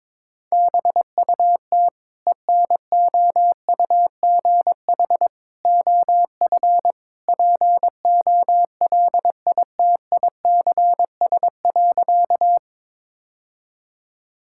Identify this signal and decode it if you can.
Morse code, heard as a keyed tone, about 22 wpm, 700 Hz. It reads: BUT ENOUGH OF POLITICS.
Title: Sentences from Top 1000 Words - Review Part 6 - 22wpm